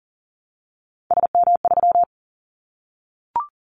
Morse code, heard as a keyed tone, about 40 wpm, 700 Hz. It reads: SM3
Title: Callsign Prefixes for Top 25 DXCC - Repeated 3x - 40wpm